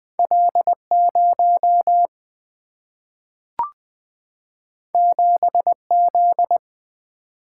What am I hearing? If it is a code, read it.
Morse code, heard as a keyed tone, about 20 wpm, 700 Hz. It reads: L 0 7 Z
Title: Single-Letter Single-Number 20wpm